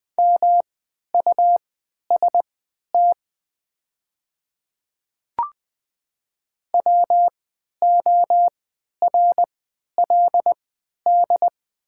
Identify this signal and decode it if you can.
Morse code, heard as a keyed tone, about 20 wpm, spaced as for 11 wpm, 700 Hz. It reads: MUST WORLD
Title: 3x Character Spacing - Top 500 Words - 20wpm